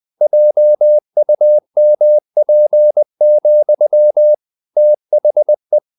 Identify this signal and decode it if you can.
Morse code, heard as a keyed tone, about 20 wpm, 600 Hz. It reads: JUMP, THE
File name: the old man and the sea / the old man and the sea part 1 20wpm